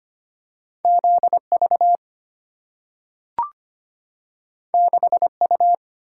Morse code, heard as a keyed tone, about 25 wpm, 700 Hz. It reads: ZV 6U
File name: Two Letter-Number Combinations - 25wpm